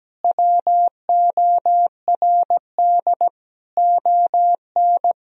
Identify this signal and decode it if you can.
Morse code, heard as a keyed tone, about 17 wpm, 700 Hz. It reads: WORD ON
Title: Sentences from Top 1000 Words - Review Part 4 - 17wpm